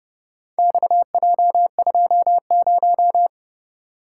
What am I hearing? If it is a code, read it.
Morse code, heard as a keyed tone, about 26 wpm, 700 Hz. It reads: XJ20